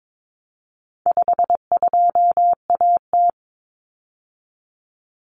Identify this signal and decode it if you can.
Morse code, heard as a keyed tone, about 22 wpm, 700 Hz. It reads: 52AT
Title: Common Cut Numbers - 4-Digit 22wpm